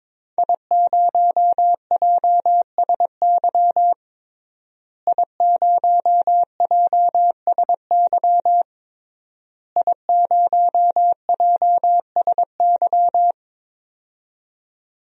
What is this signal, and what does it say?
Morse code, heard as a keyed tone, about 22 wpm, 700 Hz. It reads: I0JSY I0JSY I0JSY
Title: DXCC-100 Call Signs - 5 Characters - Repeated 3x - 22wpm